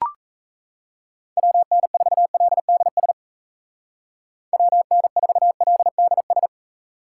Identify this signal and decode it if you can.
Morse code, heard as a keyed tone, about 34 wpm, 700 Hz. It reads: WN4LDS WN4LDS
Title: Speed Racing - USA Call Signs 28wpm